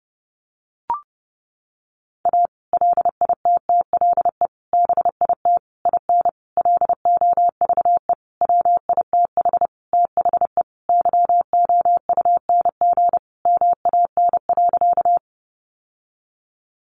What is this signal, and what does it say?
Morse code, heard as a keyed tone, about 30 wpm, 700 Hz. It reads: A LITTLE BIT IN LOVE WITH THE YOUNG MAN.
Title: Sentences from Top 900 Words 30wpm